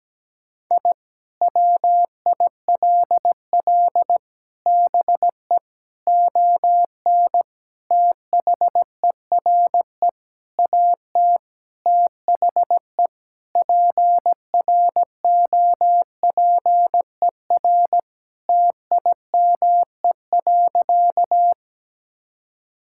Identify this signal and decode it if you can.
Morse code, heard as a keyed tone, about 17 wpm, 700 Hz. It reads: I WILL BE ON THERE AT THE PROPER TIME.
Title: Sentences from Top 1000 Words - Review Part 5 - 17wpm